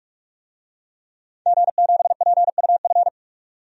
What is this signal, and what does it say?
Morse code, heard as a keyed tone, about 46 wpm, 700 Hz. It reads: G7PUF
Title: DXCC-100 Call Signs - 5 Characters - 45wpm